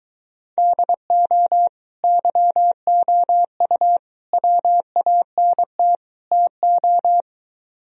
Morse code, heard as a keyed tone, about 23 wpm, 700 Hz. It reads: DO YOU WANT TO